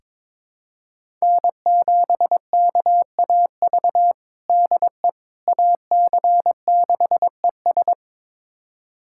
Morse code, heard as a keyed tone, about 22 wpm, 700 Hz. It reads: N7KAV DE AC6ES
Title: USA Callsign DE Callsign - 5 Characters - 22wpm